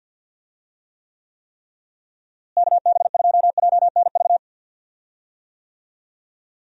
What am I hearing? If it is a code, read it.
Morse code, heard as a keyed tone, about 47 wpm, 700 Hz. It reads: KB2JNV
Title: USA Call Signs - 6 Characters - Repeated 3x - 50wpm